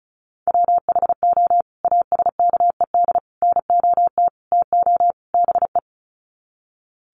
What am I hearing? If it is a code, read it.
Morse code, heard as a keyed tone, about 34 wpm, 700 Hz. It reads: WHO ASKED NOT TO BE
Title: Sets of 5 Words Encore2 35wpm